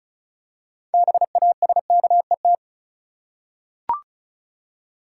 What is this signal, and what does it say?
Morse code, heard as a keyed tone, about 36 wpm, 700 Hz. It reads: BASKET
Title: Six Letter Words - 35wpm